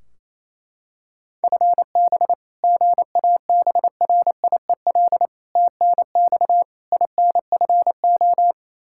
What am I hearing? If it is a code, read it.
Morse code, heard as a keyed tone, about 28 wpm, 700 Hz. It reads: FB GABRIEL TNX INFO
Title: Practice QSO - Exchange 3 - More Info - 28wpm